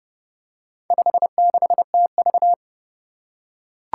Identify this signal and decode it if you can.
Morse code, heard as a keyed tone, about 30 wpm, 700 Hz. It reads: 56TV
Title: Four Challenging Letter-Number Combinations 30wpm